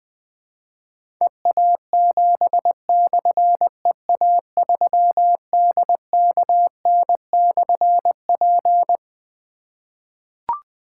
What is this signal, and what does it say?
Morse code, heard as a keyed tone, about 20 wpm, 700 Hz. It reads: EA7/EA3DKN/P
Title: Difficult Call Signs 20wpm